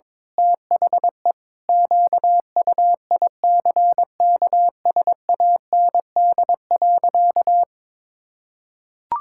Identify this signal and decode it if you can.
Morse code, heard as a keyed tone, about 22 wpm, 700 Hz. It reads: THE QUICKSAND.
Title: Pangram Challenge v2 - 22wpm